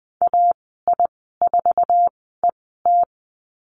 Morse code, heard as a keyed tone, about 20 wpm, spaced as for 15 wpm, 700 Hz. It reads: AI4ET
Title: 193 Lesson 31g - Letter v - Call Signs 20wpm